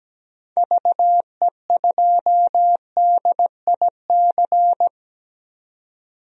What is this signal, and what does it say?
Morse code, heard as a keyed tone, about 17 wpm, 700 Hz. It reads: VE2DIC
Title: DXCC-100 Call Signs - 6 Characters - 17wpm